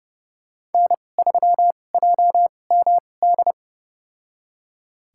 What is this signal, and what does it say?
Morse code, heard as a keyed tone, about 31 wpm, spaced as for 22 wpm, 690 Hz. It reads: N3JMD
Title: USA Call Signs - 5 Characters - 2x Character Spacing - 30wpm